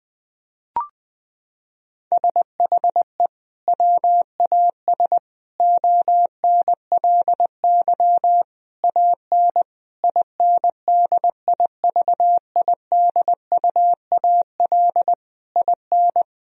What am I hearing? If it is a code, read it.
Morse code, heard as a keyed tone, about 21 wpm, 700 Hz. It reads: SHE WAS ONLY AN INDIVIDUAL IN
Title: Sentences from Top 1000 Words - Review Part 4 - 20wpm